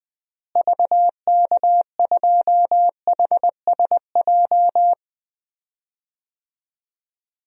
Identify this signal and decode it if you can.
Morse code, heard as a keyed tone, about 20 wpm, 700 Hz. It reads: VK2HSJ
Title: DXCC-100 Call Signs - 6 Characters - Repeated 3x - 20wpm